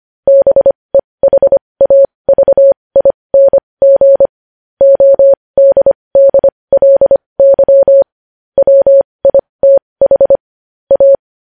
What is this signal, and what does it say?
Morse code, heard as a keyed tone, about 25 wpm, spaced as for 20 wpm, 550 Hz. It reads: BEHAVING ODDLY WITH A
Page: news headlines in morse code at 20 wpm